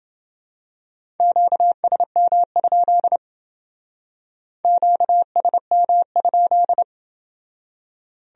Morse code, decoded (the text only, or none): QSM? QSM?